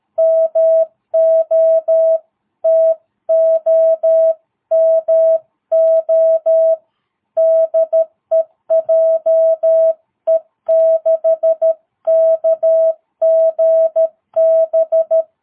Morse code, heard as a keyed tone, about 13 wpm, 650 Hz. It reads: MO TOMO DE JE6KGB
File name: jh8cbh / 2022cw